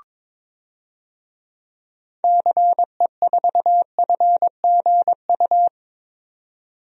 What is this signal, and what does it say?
Morse code, heard as a keyed tone, about 22 wpm, 700 Hz. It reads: CE4FGU
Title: DXCC-100 Call Signs - 6 Characters - 22wpm